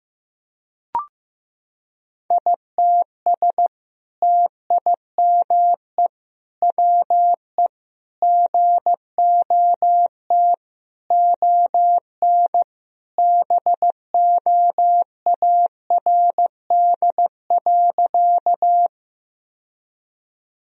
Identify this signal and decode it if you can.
Morse code, heard as a keyed tone, about 15 wpm, 700 Hz. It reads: ITS TIME WE GOT ON BOARD.
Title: Sentences from Top 1000 Words - Review Part 4 - 15wpm